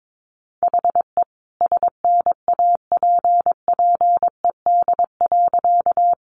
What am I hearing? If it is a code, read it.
Morse code, heard as a keyed tone, about 22 wpm, 700 Hz. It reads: HE SNAPPED.
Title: A Sound Of Thunder 22wpm